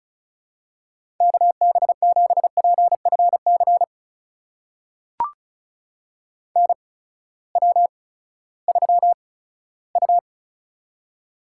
Morse code, heard as a keyed tone, about 36 wpm, spaced as for 18 wpm, 700 Hz. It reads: KB7PFC NW3U
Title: USA Call Signs - 8x Character Spacing - 35wpm